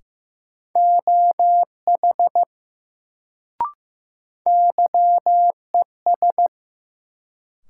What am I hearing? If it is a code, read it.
Morse code, heard as a keyed tone, about 15 wpm, 700 Hz. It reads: OH YES
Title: Sentences from Top 500 Words - New Word Drill - 15wpm